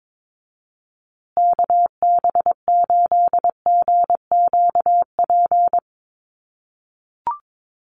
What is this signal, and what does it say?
Morse code, heard as a keyed tone, about 22 wpm, 700 Hz. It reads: KB8GQP
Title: USA Call Signs Repeated 3x 22wpm